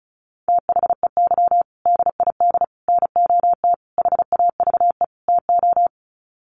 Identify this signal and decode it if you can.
Morse code, heard as a keyed tone, about 34 wpm, 700 Hz. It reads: THEY DID NOT HAVE TO
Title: Sets of 5 Words Encore2 35wpm